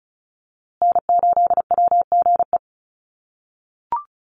Morse code, heard as a keyed tone, about 34 wpm, 700 Hz. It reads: N8WGE
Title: USA Call Signs Repeated 3x 35wpm